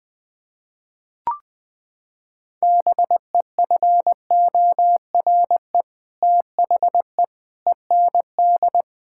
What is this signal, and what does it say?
Morse code, heard as a keyed tone, about 21 wpm, 700 Hz. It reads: BEFORE THE END
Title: Sentences from Top 1000 Words - Review Part 5 - 20wpm